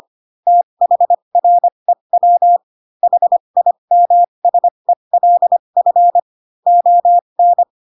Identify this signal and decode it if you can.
Morse code, heard as a keyed tone, about 25 wpm, spaced as for 22 wpm, 700 Hz. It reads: THREW HIMSELF ON
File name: Pinocchio / Pinocchio ch4 18 25 ik1hge